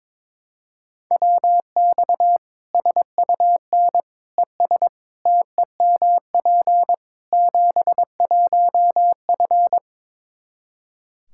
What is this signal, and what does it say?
Morse code, heard as a keyed tone, about 22 wpm, 700 Hz. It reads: WX SUN ES TEMP 71F
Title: Practice QSO - Exchange 2 - Info - 22wpm